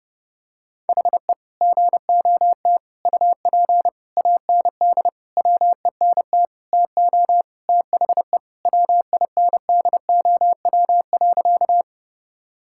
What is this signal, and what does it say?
Morse code, heard as a keyed tone, about 30 wpm, 700 Hz. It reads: HE GOT UP AND WENT TO THE WINDOW.
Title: Sentences from Top 1600 Words 30wpm